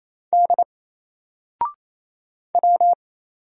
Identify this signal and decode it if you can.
Morse code, heard as a keyed tone, about 28 wpm, 700 Hz. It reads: D W
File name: Single-Letter-Number-28wpm